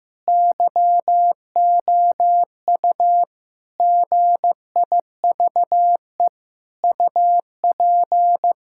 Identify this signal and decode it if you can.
Morse code, heard as a keyed tone, about 15 wpm, 700 Hz. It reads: YOU GIVE UP
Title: Sentences from Top 1000 Words - Review Part 5 - 15wpm